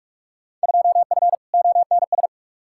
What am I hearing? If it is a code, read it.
Morse code, heard as a keyed tone, about 47 wpm, 700 Hz. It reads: 2F ONS
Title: Field Day Exchanges - 45wpm